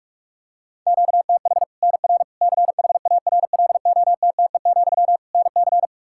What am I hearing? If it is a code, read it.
Morse code, heard as a keyed tone, about 41 wpm, 700 Hz. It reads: QTH NR CHARLOTTE, NC
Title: Practice QSO - Exchange 1 - Beginning - 45wpm